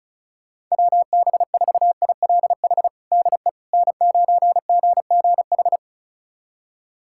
Morse code, heard as a keyed tone, about 36 wpm, 700 Hz. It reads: WB4ILH DE N9GGH
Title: USA Callsign DE Callsign 35wpm